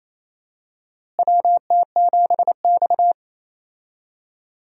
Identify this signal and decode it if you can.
Morse code, heard as a keyed tone, about 28 wpm, 700 Hz. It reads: WT7X
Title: USA Call Signs - 4 Characters - Repeated 3x - 28wpm